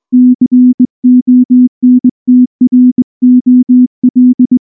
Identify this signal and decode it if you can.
Morse code, heard as a keyed tone, about 19 wpm, 250 Hz. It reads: CONTROL
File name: morse / morse-training-20WPM-7